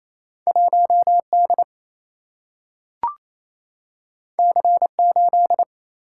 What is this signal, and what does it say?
Morse code, heard as a keyed tone, about 28 wpm, 700 Hz. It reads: 1D C8
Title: Single-Letter Single-Number 28wpm